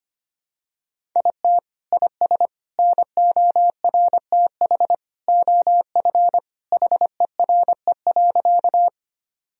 Morse code, heard as a keyed tone, about 25 wpm, 700 Hz. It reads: IT IS NORTH OF HERE.
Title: Sentences from Top 1000 Words - Review Part 1 - 25wpm